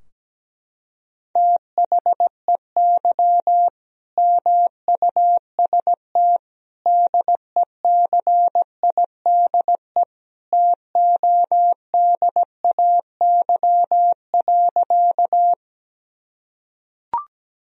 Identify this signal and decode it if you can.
Morse code, heard as a keyed tone, about 17 wpm, 700 Hz. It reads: THEY MUST DECIDE TODAY.
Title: Sentences from Top 1000 Words - Review Part 3 - 17wpm